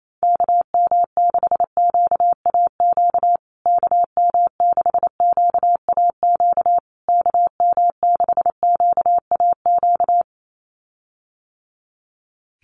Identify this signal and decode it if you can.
Morse code, heard as a keyed tone, about 28 wpm, 700 Hz. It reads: KM6QAQ KM6QAQ KM6QAQ
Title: USA Call Signs Repeated 3x 28wpm